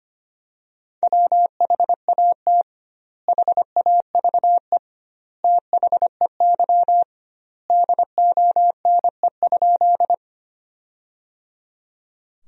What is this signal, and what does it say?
Morse code, heard as a keyed tone, about 25 wpm, 700 Hz. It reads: WHAT HAVE THEY DONE?
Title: 2x Word-Spacing - Sentences from Top 500 Words - 25wpm